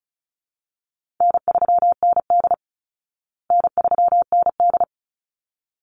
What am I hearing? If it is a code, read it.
Morse code, heard as a keyed tone, about 34 wpm, 700 Hz. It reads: N3ND N3ND
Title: Easy Call Signs Repeated 2x - 35wpm